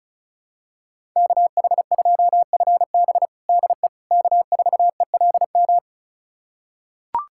Morse code, heard as a keyed tone, about 36 wpm, 700 Hz. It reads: KH2FB DE K4ELM
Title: USA Callsign DE Callsign - 5 Characters - 35wpm